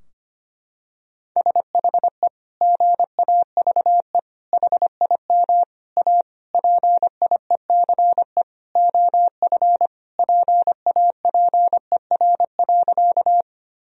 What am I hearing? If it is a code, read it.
Morse code, heard as a keyed tone, about 25 wpm, 700 Hz. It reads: SHE GAVE HIM A PIECE OF PAPER.
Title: Sentences from Top 1000 Words - Review Part 2 - 25wpm